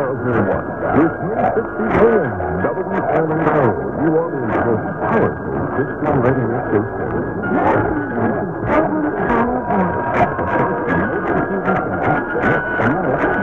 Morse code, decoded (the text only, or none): none